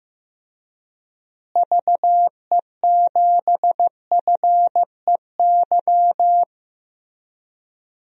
Morse code, heard as a keyed tone, about 15 wpm, 700 Hz. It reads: VE7FEY
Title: DXCC-100 Call Signs - 6 Characters - Repeated 3x - 15wpm